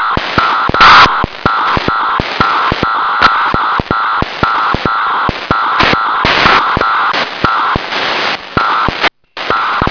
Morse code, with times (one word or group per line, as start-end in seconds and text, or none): none